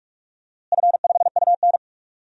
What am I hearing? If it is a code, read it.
Morse code, heard as a keyed tone, about 42 wpm, 700 Hz. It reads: F5UN
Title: DXCC-100 Call Signs - 4 Characters - Repeated 2x - 45wpm